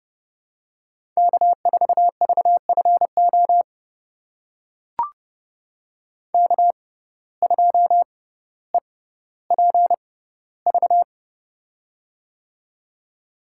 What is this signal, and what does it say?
Morse code, heard as a keyed tone, about 30 wpm, spaced as for 15 wpm, 700 Hz. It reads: K4VFO K2EPV
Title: USA Call Signs - 5 Characters - 6x Character Spacing - 30wpm